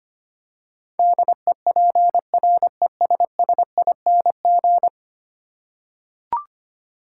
Morse code, heard as a keyed tone, about 25 wpm, 700 Hz. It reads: DEPRESSING
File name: Ten Letter Words - 25wpm